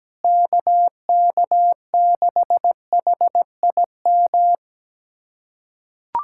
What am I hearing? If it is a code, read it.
Morse code, heard as a keyed tone, about 17 wpm, 700 Hz. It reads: KK6HIM
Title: USA Callsign DE Callsign - 6 Characters - 17wpm